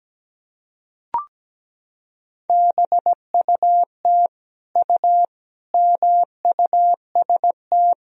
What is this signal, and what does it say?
Morse code, heard as a keyed tone, about 17 wpm, 700 Hz. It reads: BUT U MUST
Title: Sentences from Top 300 Words and Abbreviations 17wpm